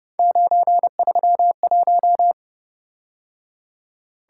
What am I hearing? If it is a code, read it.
Morse code, heard as a keyed tone, about 30 wpm, 700 Hz. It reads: 931